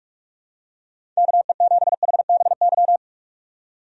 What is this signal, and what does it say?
Morse code, heard as a keyed tone, about 45 wpm, 700 Hz. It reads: KE7HBY
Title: USA Call Signs - 6 Characters - 45wpm